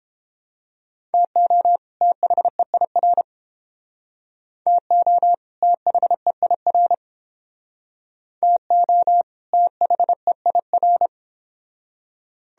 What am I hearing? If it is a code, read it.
Morse code, heard as a keyed tone, about 30 wpm, 700 Hz. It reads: TO THEIR TO THEIR TO THEIR